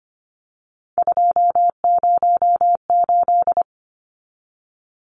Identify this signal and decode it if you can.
Morse code, heard as a keyed tone, about 25 wpm, 700 Hz. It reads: 208
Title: Numbers - 3-Digit 25wpm